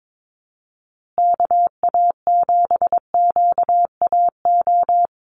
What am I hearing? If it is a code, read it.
Morse code, heard as a keyed tone, about 22 wpm, 700 Hz. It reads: KA7QAO
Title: USA Call Signs Repeated 2x 22wpm